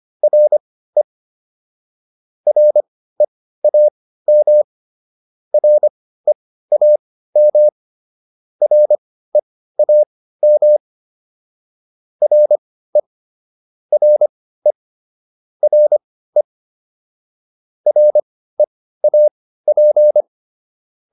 Morse code, heard as a keyed tone, about 25 wpm, spaced as for 15 wpm, 600 Hz. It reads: RE REAM REAM REAM RE RE RE REAP
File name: RE4 15